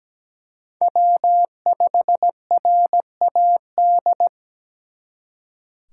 Morse code, heard as a keyed tone, about 21 wpm, 700 Hz. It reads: W5RAD
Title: Speed Racing - USA 1x3 Call Signs 17wpm